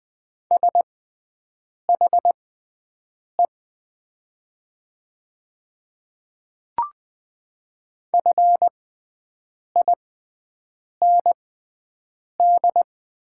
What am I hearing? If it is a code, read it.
Morse code, heard as a keyed tone, about 20 wpm, spaced as for 7 wpm, 700 Hz. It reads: SHE FIND
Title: 6x Character Spacing - Top 100 Words - 20wpm